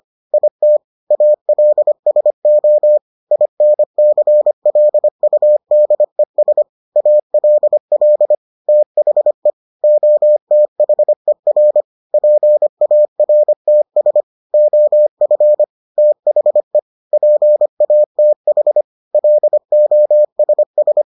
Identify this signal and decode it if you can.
Morse code, heard as a keyed tone, about 25 wpm, 600 Hz. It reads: IT ALSO INCLUDES ALL THE OTHER PARTS OF THE PATH LOSS